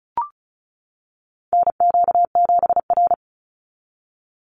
Morse code, heard as a keyed tone, about 34 wpm, 700 Hz. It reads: NQ7R
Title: NA SOTA Chaser Callsigns 35wpm